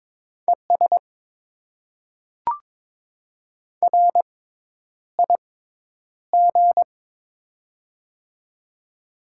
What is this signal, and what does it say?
Morse code, heard as a keyed tone, about 23 wpm, spaced as for 10 wpm, 690 Hz. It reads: ES RIG
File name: Top 20 QSO Elements - 6x Character Spacing - 22wpm